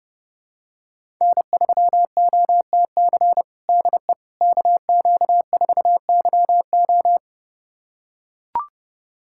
Morse code, heard as a keyed tone, about 32 wpm, 700 Hz. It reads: N3OTC DE KQ4YO
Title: USA Callsign DE Callsign - 5 Characters - 30wpm